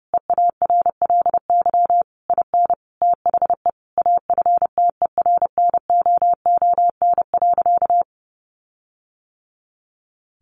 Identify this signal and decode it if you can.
Morse code, heard as a keyed tone, about 30 wpm, 700 Hz. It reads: EARLY IN THE AFTERNOON.
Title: Sentences from Top 1500 Words 30wpm